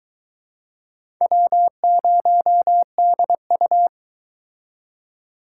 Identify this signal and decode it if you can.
Morse code, heard as a keyed tone, about 23 wpm, 700 Hz. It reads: W0DU